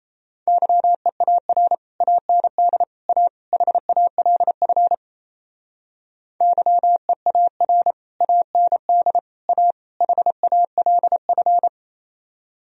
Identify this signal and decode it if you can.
Morse code, heard as a keyed tone, about 37 wpm, 700 Hz. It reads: YEAR AND A HALF YEAR AND A HALF